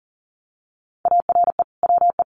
Morse code, heard as a keyed tone, about 40 wpm, 700 Hz. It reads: ARE WE
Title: Sentences from Top 1400 Words 40wpm